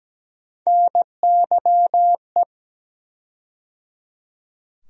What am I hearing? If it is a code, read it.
Morse code, heard as a keyed tone, about 17 wpm, 700 Hz. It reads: NYE
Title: USA Call Sign Postfixes - 3 Letters 17wpm